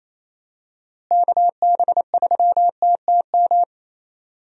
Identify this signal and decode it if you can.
Morse code, heard as a keyed tone, about 28 wpm, 700 Hz. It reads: KB3TTM
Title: USA Call Signs - 6 Characters - Repeated 3x - 28wpm